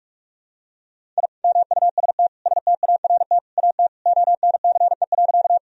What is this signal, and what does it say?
Morse code, heard as a keyed tone, about 42 wpm, 700 Hz. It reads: I MUST START AT ONCE.
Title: Sentences from Top 1000 Words - Review Part 1 - 45wpm